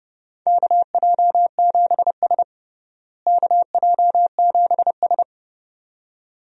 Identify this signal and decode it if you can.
Morse code, heard as a keyed tone, about 29 wpm, 700 Hz. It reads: KJ7S KJ7S
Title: USA Call Signs - 4 Characters - Repeated 2x - 30wpm